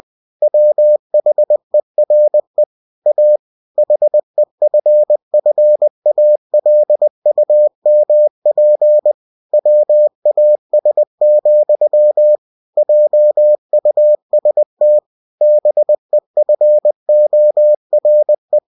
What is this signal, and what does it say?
Morse code, heard as a keyed tone, about 20 wpm, 600 Hz. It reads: WHERE A HEFFALUMP WAS, JUST BEFORE
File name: cw-books / winnie-the-pooh / winnie-20wpm-page-0036